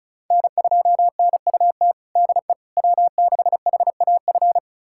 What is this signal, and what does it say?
Morse code, heard as a keyed tone, about 36 wpm, 700 Hz. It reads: N2NUT DE W6HAF
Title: USA Callsign DE Callsign - 5 Characters - 35wpm